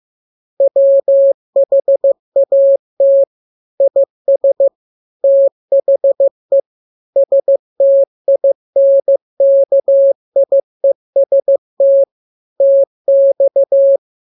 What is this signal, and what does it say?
Morse code, heard as a keyed tone, about 15 wpm, 550 Hz. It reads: WHAT IS THE STINKIEST TX
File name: jokes / joke 27 15 0000